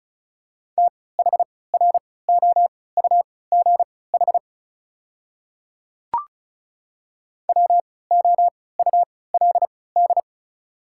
Morse code, heard as a keyed tone, about 36 wpm, spaced as for 20 wpm, 700 Hz. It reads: THROUGH WOULD
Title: 3x Character Spacing - Top 200 Words - 35wpm